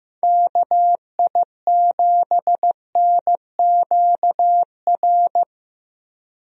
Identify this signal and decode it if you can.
Morse code, heard as a keyed tone, about 15 wpm, 700 Hz. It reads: KI7NQR
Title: USA Call Signs - 6 Characters - 15wpm